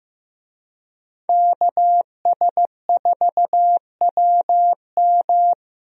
Morse code, heard as a keyed tone, about 15 wpm, 700 Hz. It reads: KS4WM
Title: USA Call Signs - 5 Characters - 15wpm